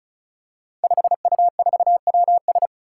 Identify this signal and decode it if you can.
Morse code, heard as a keyed tone, about 36 wpm, 700 Hz. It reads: 5U4WS
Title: Five Challenging Letter-Number Combinations 35wpm